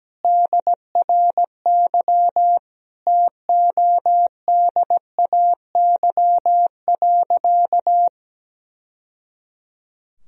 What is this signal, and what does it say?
Morse code, heard as a keyed tone, about 17 wpm, 700 Hz. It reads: DRY TODAY.
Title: Sentences from Top 1000 Words - Review Part 3 - 17wpm